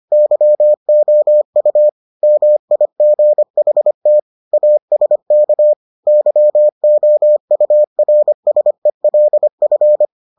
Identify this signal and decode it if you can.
Morse code, heard as a keyed tone, about 25 wpm, 600 Hz. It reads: YOU MIGHT ASK YOURSELF